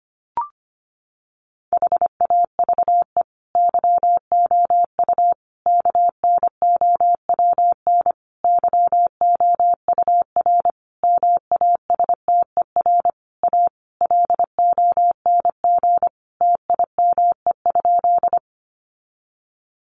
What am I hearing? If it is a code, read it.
Morse code, heard as a keyed tone, about 25 wpm, 700 Hz. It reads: HAVE YOU KNOWN YOUR MASTER A LONG TIME?
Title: Sentences from Top 1300 Words 25wpm